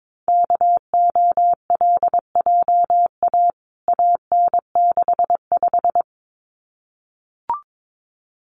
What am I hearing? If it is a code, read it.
Morse code, heard as a keyed tone, about 22 wpm, 700 Hz. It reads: KOLJA AN65
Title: CWT Exchange 22wpm